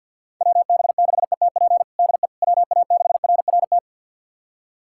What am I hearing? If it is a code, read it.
Morse code, heard as a keyed tone, about 50 wpm, 700 Hz. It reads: WB6ETP DE WA6RRT
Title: USA Callsign DE Callsign - 6 Characters - 50wpm